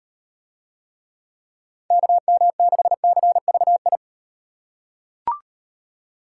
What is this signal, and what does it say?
Morse code, heard as a keyed tone, about 37 wpm, 700 Hz. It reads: KM6CVI